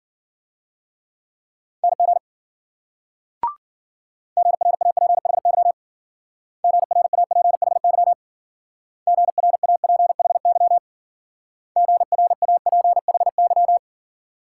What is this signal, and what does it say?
Morse code, heard as a keyed tone, about 52 wpm, 700 Hz. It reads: NG GRAPHY GRAPHY GRAPHY GRAPHY